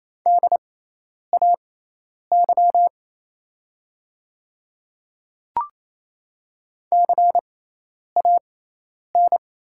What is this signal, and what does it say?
Morse code, heard as a keyed tone, about 28 wpm, spaced as for 10 wpm, 700 Hz. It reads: DAY CAN
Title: 6x Character Spacing - Top 200 Words - 28wpm